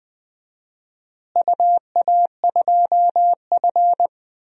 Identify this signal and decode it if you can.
Morse code, heard as a keyed tone, about 20 wpm, 700 Hz. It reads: UA2F
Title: DXCC-100 Call Signs - 4 Characters - Repeated 3x - 20wpm